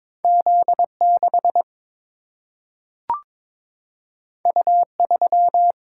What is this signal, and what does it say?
Morse code, heard as a keyed tone, about 22 wpm, 700 Hz. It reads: Z6 U3
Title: Single-Letter Single-Number 22wpm